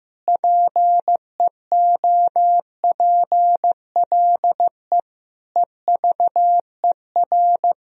Morse code, heard as a keyed tone, about 15 wpm, 700 Hz. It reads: PEOPLE EVER
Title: Sentences from Top 1000 Words - Review Part 1 - 15wpm